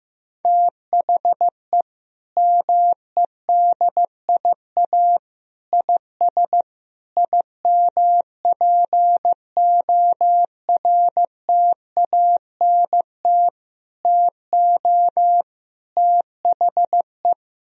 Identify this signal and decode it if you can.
Morse code, heard as a keyed tone, about 15 wpm, 700 Hz. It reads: THE MEDIA IS IMPORTANT TO THE
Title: Sentences from Top 1000 Words - Review Part 4 - 15wpm